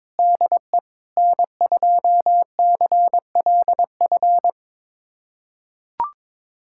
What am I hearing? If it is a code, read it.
Morse code, heard as a keyed tone, about 21 wpm, 700 Hz. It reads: DE N2CLF
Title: USA Callsign DE Callsign - 5 Characters - 22wpm